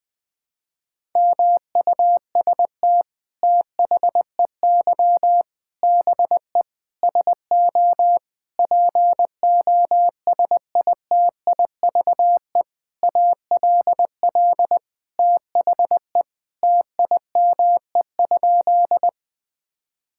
Morse code, heard as a keyed tone, about 20 wpm, 700 Hz. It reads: MUST THEY BE SO POSITIVE ALL THE TIME?
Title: Sentences from Top 1000 Words - Review Part 5 - 20wpm